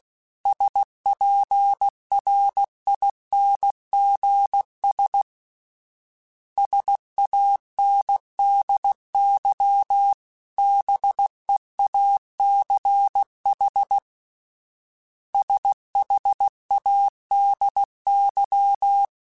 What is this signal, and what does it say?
Morse code, heard as a keyed tone, about 16 wpm, 800 Hz. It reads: SPRINGS SANDY BEACH SHADY
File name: fromSMs / FromSMs mp3s / Cities@15